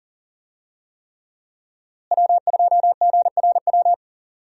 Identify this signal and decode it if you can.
Morse code, heard as a keyed tone, about 40 wpm, 700 Hz. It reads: W2GRW
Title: USA Call Signs - 5 Characters - Repeated 3x - 40wpm